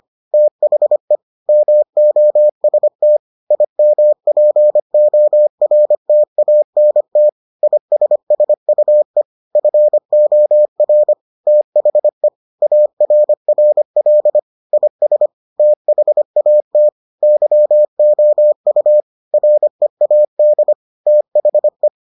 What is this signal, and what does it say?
Morse code, heard as a keyed tone, about 25 wpm, 600 Hz. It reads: THE MOST IMPORTANT ISSUE FOR THE ARRL IS THAT YOU READ THE